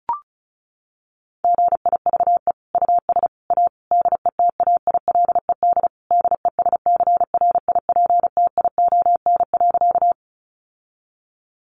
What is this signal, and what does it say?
Morse code, heard as a keyed tone, about 34 wpm, 700 Hz. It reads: GIVE US A DETAILED DESCRIPTION.
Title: Sentences from Top 1500 Words 35wpm